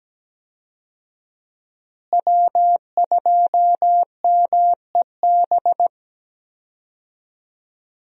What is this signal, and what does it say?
Morse code, heard as a keyed tone, about 17 wpm, 700 Hz. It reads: W2MEB
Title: USA Call Signs - 5 Characters - Repeated 2x - 17wpm